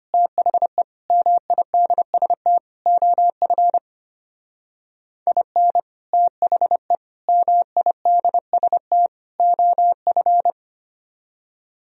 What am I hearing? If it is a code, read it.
Morse code, heard as a keyed tone, about 33 wpm, 700 Hz. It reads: THE MIDST OF IN THE MIDST OF